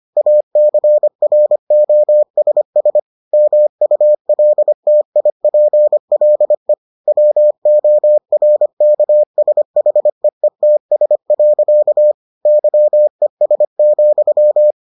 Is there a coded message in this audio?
Yes: Morse code from 0.2 to 14.7 s, heard as a keyed tone, about 25 wpm, 600 Hz. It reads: ACROSS MULTIPLE WORKSHEETS. YES,